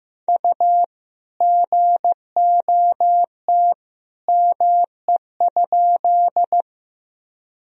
Morse code, heard as a keyed tone, about 15 wpm, 700 Hz. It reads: U GOT ME?
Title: Sentences from Top 300 Words and Abbreviations 15wpm